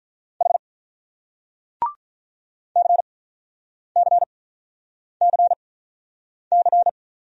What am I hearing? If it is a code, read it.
Morse code, heard as a keyed tone, about 44 wpm, spaced as for 9 wpm, 690 Hz. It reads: H CCCC
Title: Speed-Racing Single Letters 35wpm